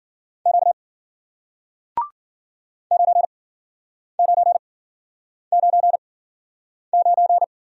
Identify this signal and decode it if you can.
Morse code, heard as a keyed tone, about 53 wpm, spaced as for 9 wpm, 700 Hz. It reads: Q 9999